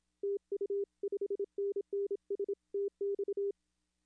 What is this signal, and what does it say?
Morse code, heard as a keyed tone, about 27 wpm, 410 Hz. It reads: TU 5NN STX
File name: contests / arrl-160-k5tr-op-w5tn -0300